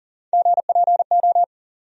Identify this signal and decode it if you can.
Morse code, heard as a keyed tone, about 40 wpm, 700 Hz. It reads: GPO
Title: USA Call Sign Postfixes - 3 Letters 40wpm